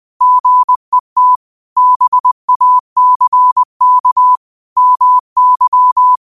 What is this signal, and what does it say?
Morse code, heard as a keyed tone, about 20 wpm, 1000 Hz. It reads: GET BACK MY
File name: qotdfiles / 20wpm